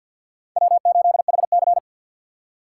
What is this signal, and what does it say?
Morse code, heard as a keyed tone, about 53 wpm, 700 Hz. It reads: W8HC
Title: USA Call Signs - 4x Character Spacing - 50wpm